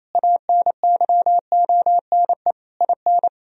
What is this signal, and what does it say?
Morse code, heard as a keyed tone, about 28 wpm, 700 Hz. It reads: ANYONE IN
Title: Sentences from Top 1000 Words - Review Part 4 - 28wpm